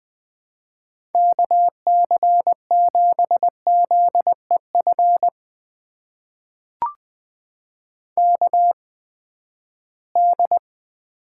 Morse code, heard as a keyed tone, about 20 wpm, spaced as for 14 wpm, 700 Hz. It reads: KC7ZEF KD
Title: USA Call Signs - 6 Characters - 8x Character Spacing - 20wpm